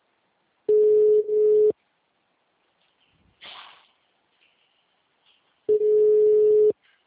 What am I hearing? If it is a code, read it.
Ringback tone (Europe, 425 Hz).
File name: out- -2034- - - .3560